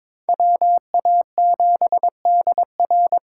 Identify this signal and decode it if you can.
Morse code, heard as a keyed tone, about 26 wpm, 700 Hz. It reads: WA7DR
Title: Speed Racing - USA 2x2 Call Signs 22wpm